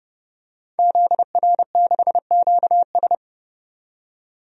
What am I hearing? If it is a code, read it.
Morse code, heard as a keyed tone, about 31 wpm, 700 Hz. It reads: ZR6QS
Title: DXCC-100 Call Signs - 5 Characters - 30wpm